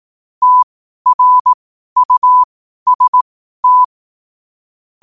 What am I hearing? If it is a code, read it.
Morse code, heard as a keyed tone, about 18 wpm, spaced as for 12 wpm, 1000 Hz. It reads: TRUST
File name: qotdfiles / 12wpm